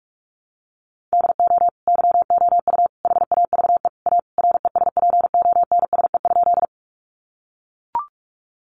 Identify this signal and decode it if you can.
Morse code, heard as a keyed tone, about 44 wpm, 700 Hz. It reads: DO YOU HAVE A RESPONSE?
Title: Sentences from Top 1400 Words 45wpm